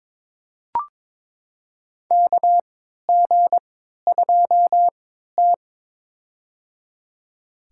Morse code, heard as a keyed tone, about 22 wpm, spaced as for 13 wpm, 700 Hz. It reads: KG2T
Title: USA Call Signs - 4 Characters - 3x Character Spacing - 22wpm